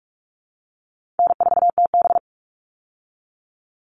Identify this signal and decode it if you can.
Morse code, heard as a keyed tone, about 44 wpm, 700 Hz. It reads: N4TB
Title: Call Signs Repeated 2x - 45wpm